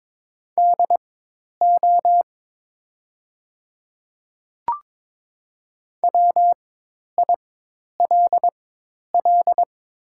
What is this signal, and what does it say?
Morse code, heard as a keyed tone, about 22 wpm, spaced as for 10 wpm, 700 Hz. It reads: DO WILL
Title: 4x Character Spacing - Top 200 Words - 22wpm